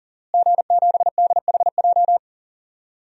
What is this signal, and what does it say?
Morse code, heard as a keyed tone, about 40 wpm, 700 Hz. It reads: G7DHJ
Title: DXCC-100 Call Signs - 5 Characters - Repeated 3x - 40wpm